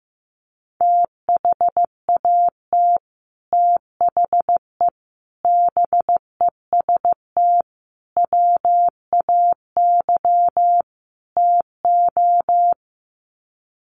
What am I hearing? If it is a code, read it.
Morse code, heard as a keyed tone, about 15 wpm, 700 Hz. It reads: THAT THE BEST WAY TO
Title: Sets of 5 Words Encore2 15wpm